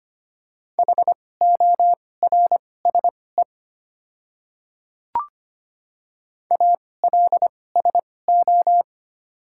Morse code, heard as a keyed tone, about 25 wpm, spaced as for 18 wpm, 690 Hz. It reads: HORSE ALSO
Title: 2x Character Spacing - Top 500 Words - 25wpm